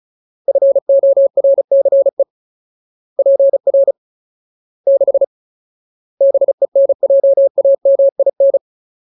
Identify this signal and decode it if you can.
Morse code, heard as a keyed tone, about 34 wpm, 550 Hz. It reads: FORCE PR 6 BENJAMIN
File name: Ultimate Word List / 35WPM Ultimate